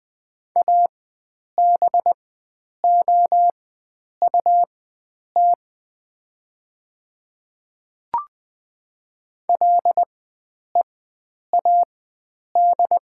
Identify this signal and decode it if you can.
Morse code, heard as a keyed tone, about 20 wpm, spaced as for 9 wpm, 700 Hz. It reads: ABOUT LEAD
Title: 4x Character Spacing - Top 500 Words - 20wpm